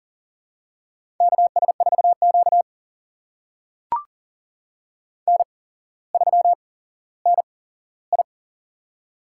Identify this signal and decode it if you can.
Morse code, heard as a keyed tone, about 40 wpm, spaced as for 17 wpm, 700 Hz. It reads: KS4Q N3NI
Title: USA Call Signs - 4 Characters - 8x Character Spacing - 40wpm